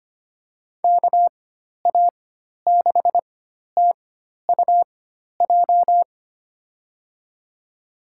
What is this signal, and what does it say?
Morse code, heard as a keyed tone, about 25 wpm, spaced as for 12 wpm, 700 Hz. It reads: KA6TUJ
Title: USA Call Signs - 6 Characters - 4x Character Spacing - 25wpm